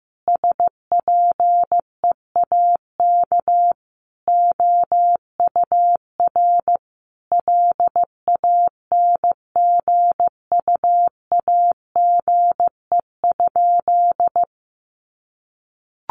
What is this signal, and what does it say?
Morse code, heard as a keyed tone, about 15 wpm, 700 Hz. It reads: SPEAK OUR LANGUAGE?
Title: Sentences from Top 1100 Words 15wpm